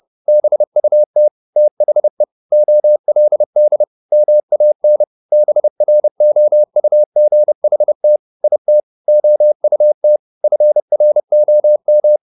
Morse code, heard as a keyed tone, about 30 wpm, 600 Hz. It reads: BUT THE OLD MAN BROUGHT IT OUT FROM
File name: the old man and the sea / the old man and the sea 1 30wpm